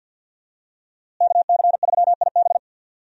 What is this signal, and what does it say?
Morse code, heard as a keyed tone, about 47 wpm, 700 Hz. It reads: KC3IB
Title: USA Call Signs - 5 Characters - Repeated 3x - 50wpm